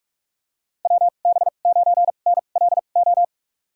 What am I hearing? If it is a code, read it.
Morse code, heard as a keyed tone, about 46 wpm, spaced as for 33 wpm, 700 Hz. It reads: WB9NLO
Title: USA Call Signs - 6 Characters - 2x Character Spacing - 45wpm